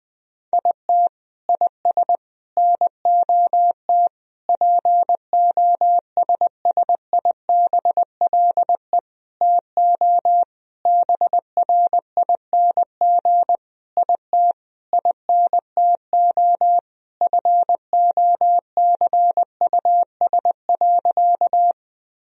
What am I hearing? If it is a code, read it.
Morse code, heard as a keyed tone, about 21 wpm, 700 Hz. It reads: IT IS NOT POSSIBLE TO BRING IT INTO FOCUS.
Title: Sentences from Top 1000 Words - Review Part 4 - 20wpm